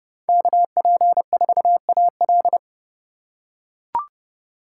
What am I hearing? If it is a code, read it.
Morse code, heard as a keyed tone, about 30 wpm, 700 Hz. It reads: KP4AL